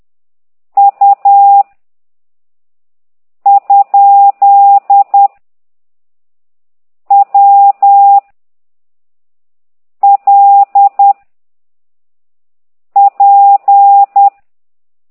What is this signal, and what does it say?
Morse code, heard as a keyed tone, about 10 wpm, 800 Hz. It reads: U ? W L P
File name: ct2idw / exerc8